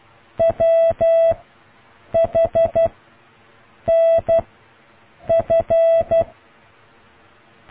Morse code, heard as a keyed tone, about 12 wpm, 660 Hz. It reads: W H N F